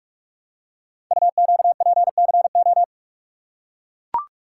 Morse code, heard as a keyed tone, about 44 wpm, 690 Hz. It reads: UQPCO
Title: Code Groups - 5 Letters - 45wpm